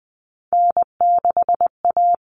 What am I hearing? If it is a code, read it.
Morse code, heard as a keyed tone, about 20 wpm, 700 Hz. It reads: N6A
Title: All Cut Numbers - 3-Digit 20wpm